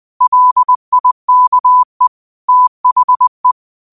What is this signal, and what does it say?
Morse code, heard as a keyed tone, about 20 wpm, 1000 Hz. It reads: LIKE THE